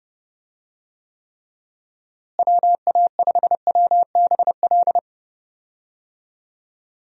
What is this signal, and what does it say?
Morse code, heard as a keyed tone, about 32 wpm, 700 Hz. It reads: WA5WBL
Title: USA Call Signs - 6 Characters - Repeated 3x - 30wpm